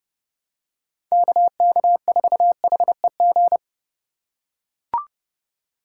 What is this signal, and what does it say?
Morse code, heard as a keyed tone, about 30 wpm, 700 Hz. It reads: KK4HEG
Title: K1USN SST QSO Practice - 30wpm